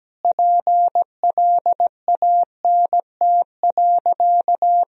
Morse code, heard as a keyed tone, about 17 wpm, 700 Hz. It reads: PLANT.
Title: Sentences from Top 1000 Words - Review Part 1 - 17wpm